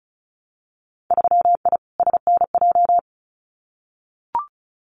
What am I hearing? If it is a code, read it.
Morse code, heard as a keyed tone, about 34 wpm, 700 Hz. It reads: 3I SNJ
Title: Winter Field Day Exchanges Encore 35wpm